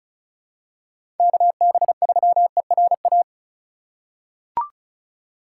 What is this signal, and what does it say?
Morse code, heard as a keyed tone, about 36 wpm, 700 Hz. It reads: KB3ERA
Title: USA Call Signs - 2x Character Spacing - 35wpm